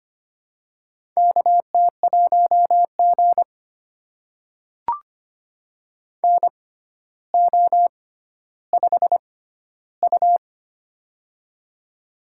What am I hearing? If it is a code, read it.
Morse code, heard as a keyed tone, about 25 wpm, spaced as for 13 wpm, 700 Hz. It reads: KT1G NO5U
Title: USA Call Signs - 4 Characters - 6x Character Spacing - 25wpm